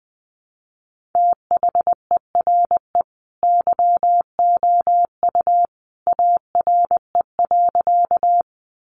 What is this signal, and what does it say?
Morse code, heard as a keyed tone, about 19 wpm, 700 Hz. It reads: THERE YOU ARE.
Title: Sentences from Top 100 Words 20wpm